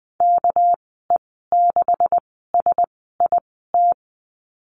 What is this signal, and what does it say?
Morse code, heard as a keyed tone, about 20 wpm, spaced as for 15 wpm, 700 Hz. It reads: KE6SIT
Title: 200 Lesson 32g - Letter k - Call Signs 20wpm